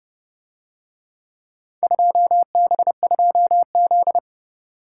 Morse code, heard as a keyed tone, about 30 wpm, 700 Hz. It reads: 2B2Z